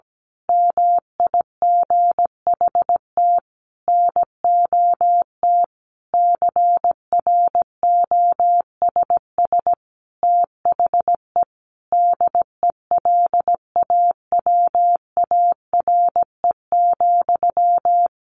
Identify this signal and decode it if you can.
Morse code, heard as a keyed tone, about 17 wpm, 700 Hz. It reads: MIGHT NOT CROSS THE DELAWARE,
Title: A Sound Of Thunder 17wpm